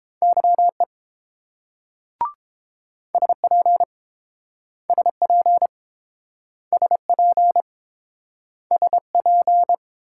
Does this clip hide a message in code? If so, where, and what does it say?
Yes: Morse code from 0.2 to 9.7 s, heard as a keyed tone, about 31 wpm, 700 Hz. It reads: YE SP SP SP SP